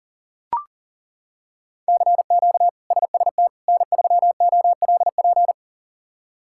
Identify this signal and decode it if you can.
Morse code, heard as a keyed tone, about 40 wpm, 700 Hz. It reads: CQ SST N3OLP
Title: K1USN SST QSO Practice - 40wpm